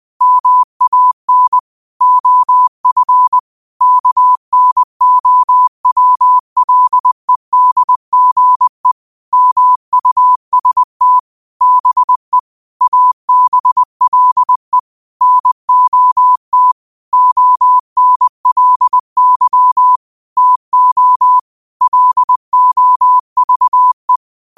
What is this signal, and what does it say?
Morse code, heard as a keyed tone, about 20 wpm, 1000 Hz. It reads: MAN OF KNOWLEDGE MUST BE ABLE NOT ONLY TO LOVE
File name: qotdfiles / 20wpm